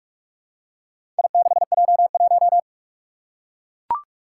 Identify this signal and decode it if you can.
Morse code, heard as a keyed tone, about 41 wpm, 700 Hz. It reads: I6J1
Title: Four Challenging Letter-Number Combinations 45wpm